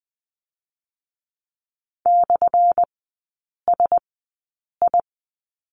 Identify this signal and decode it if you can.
Morse code, heard as a keyed tone, about 20 wpm, spaced as for 8 wpm, 700 Hz. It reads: /SI
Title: 178 Lesson 29e - Slash - Three Characters 20wpm